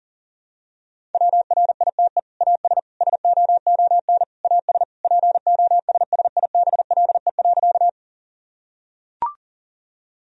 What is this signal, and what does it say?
Morse code, heard as a keyed tone, about 40 wpm, 700 Hz. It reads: WRITE AS SOON AS POSSIBLE.
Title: Sentences from Top 1000 Words - Review Part 1 - 40wpm